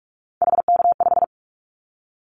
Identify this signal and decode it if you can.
Morse code, heard as a keyed tone, about 44 wpm, 700 Hz. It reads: HK5
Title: Callsign Prefixes for Top 100 DXCC - 45wpm